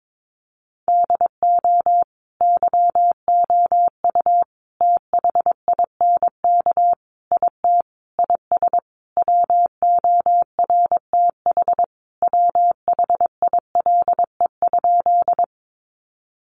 Morse code, heard as a keyed tone, about 22 wpm, 700 Hz. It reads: DO YOU THINK IT IS WORTH WHILE?
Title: Sentences from Top 800 Words 22wpm